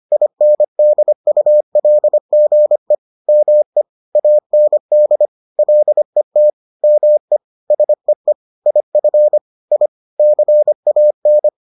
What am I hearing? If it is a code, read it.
Morse code, heard as a keyed tone, about 25 wpm, 600 Hz. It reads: INDULGE ME AND LET ME SEE IF I CAN